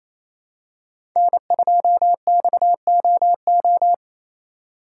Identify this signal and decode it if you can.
Morse code, heard as a keyed tone, about 28 wpm, 700 Hz. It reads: N2XOO
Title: USA Call Signs - 5 Characters - 6x Character Spacing - 28wpm